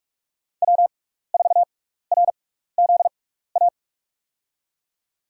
Morse code, heard as a keyed tone, about 46 wpm, spaced as for 15 wpm, 700 Hz. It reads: W4RZA
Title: USA Call Signs - 5 Characters - 6x Character Spacing - 45wpm